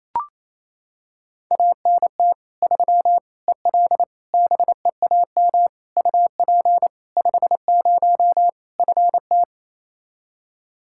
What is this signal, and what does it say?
Morse code, heard as a keyed tone, about 28 wpm, 700 Hz. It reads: ANT 3 EL BEAM UP 50 FT
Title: Practice QSO - Exchange 2 - Info - 28wpm